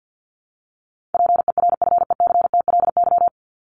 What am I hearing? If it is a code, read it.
Morse code, heard as a keyed tone, about 48 wpm, 700 Hz. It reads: PERFECTLY
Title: Top 1501-1750 Words 50wpm